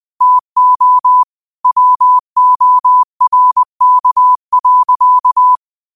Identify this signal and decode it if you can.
Morse code, heard as a keyed tone, about 20 wpm, 1000 Hz. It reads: TO WORK.
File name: qotdfiles / 20wpm